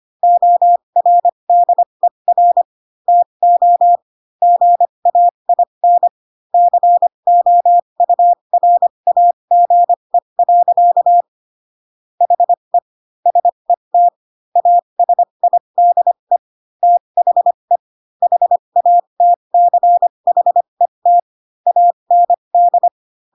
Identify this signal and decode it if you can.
Morse code, heard as a keyed tone, about 25 wpm, spaced as for 22 wpm, 700 Hz. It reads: ORDER TO GAIN COURAGE. HE SET ASIDE THE HATCHET AND